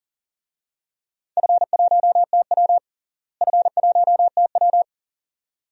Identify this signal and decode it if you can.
Morse code, heard as a keyed tone, about 40 wpm, 700 Hz. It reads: F1TW F1TW
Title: DXCC-100 Call Signs - 4 Characters - Repeated 2x - 40wpm